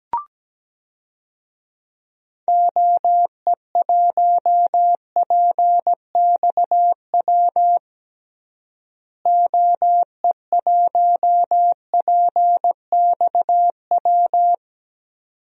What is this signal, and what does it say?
Morse code, heard as a keyed tone, about 17 wpm, 700 Hz. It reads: OE1PXW OE1PXW
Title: DXCC-100 Call Signs - 6 Characters - Repeated 2x - 17wpm